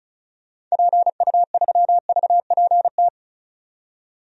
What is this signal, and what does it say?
Morse code, heard as a keyed tone, about 34 wpm, 700 Hz. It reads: PU3VPT
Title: DXCC-100 Call Signs - 6 Characters - 35wpm